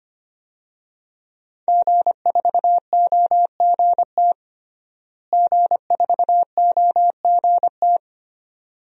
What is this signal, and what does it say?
Morse code, heard as a keyed tone, about 25 wpm, 700 Hz. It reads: G4OGT G4OGT
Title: DXCC-100 Call Signs - 5 Characters - Repeated 2x - 25wpm